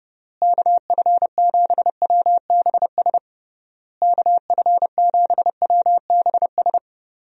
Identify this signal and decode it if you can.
Morse code, heard as a keyed tone, about 30 wpm, 700 Hz. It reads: KF7WBS KF7WBS
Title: USA Call Signs - 6 Characters - Repeated 3x - 30wpm